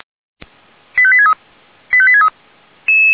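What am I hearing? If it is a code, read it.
Five-tone selective call (ZVEI), heard as quick runs of five tones, tones of about 70 ms. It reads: 85863 85863
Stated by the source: Five-tone ZVEI selective call received on an SDR.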